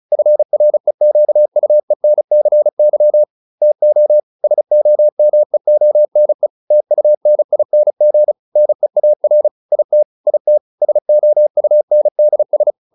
Morse code, heard as a keyed tone, about 34 wpm, 600 Hz. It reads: FREQUENCY TO SOMEONE TUNING NEAR IT IT SOUNDS